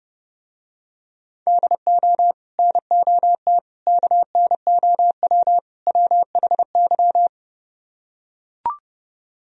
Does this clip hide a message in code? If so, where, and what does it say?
Yes: Morse code from 1.5 to 7.3 s, heard as a keyed tone, about 28 wpm, 700 Hz. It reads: DO NOT KNOW WHY